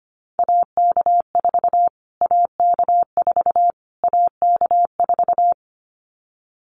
Morse code, heard as a keyed tone, about 25 wpm, 700 Hz. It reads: AK4 AK4 AK4
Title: Callsign Prefixes for Top 25 DXCC - Repeated 3x - 25wpm